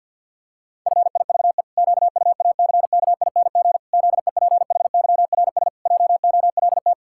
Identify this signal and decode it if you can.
Morse code, heard as a keyed tone, about 51 wpm, 700 Hz. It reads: FIVE QUACKING ZEPHYRS JOLT
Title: Pangram Challenge v2 - 50wpm